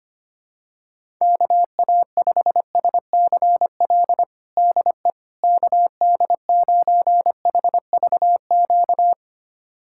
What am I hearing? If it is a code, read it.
Morse code, heard as a keyed tone, about 25 wpm, 700 Hz. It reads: KA5SCL DE KD9HVQ
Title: USA Callsign DE Callsign - 6 Characters - 25wpm